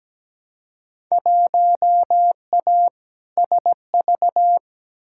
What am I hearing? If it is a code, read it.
Morse code, heard as a keyed tone, about 17 wpm, 700 Hz. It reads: 1A SV
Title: Field Day Exchanges - 17wpm